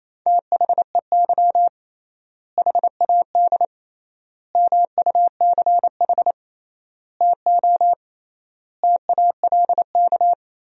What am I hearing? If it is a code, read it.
Morse code, heard as a keyed tone, about 28 wpm, 700 Hz. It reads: THEY HAD MUCH TO TALK
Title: 3x Word-Spacing - Sentences from Top 400 Words - 28wpm